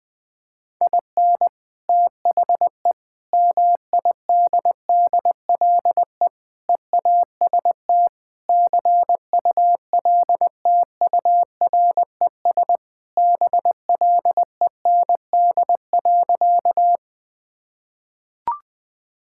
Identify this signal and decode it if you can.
Morse code, heard as a keyed tone, about 21 wpm, 700 Hz. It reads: IN THE MIDDLE EAST CULTURES BLEND.
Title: Sentences with 4-Word Phrases - 20wpm